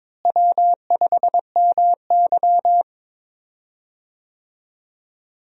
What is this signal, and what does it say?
Morse code, heard as a keyed tone, about 22 wpm, 700 Hz. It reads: W5MY
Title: USA Call Signs - 4 Characters - Repeated 3x - 22wpm